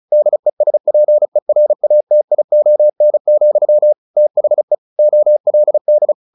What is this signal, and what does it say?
Morse code, heard as a keyed tone, about 34 wpm, 600 Hz. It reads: DESPERATION, THE OLD
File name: the old man and the sea 2 35wpm